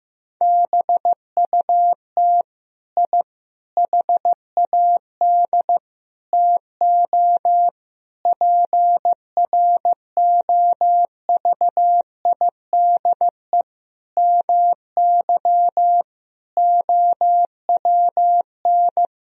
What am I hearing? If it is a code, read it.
Morse code, heard as a keyed tone, about 15 wpm, 700 Hz. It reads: BUT I HAD TO PROVIDE MY OWN
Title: Sentences from Top 1000 Words - Review Part 3 - 15wpm